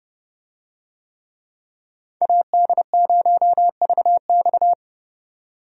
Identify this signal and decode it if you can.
Morse code, heard as a keyed tone, about 31 wpm, 700 Hz. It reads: AD0VX
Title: USA Call Signs - 5 Characters - 30wpm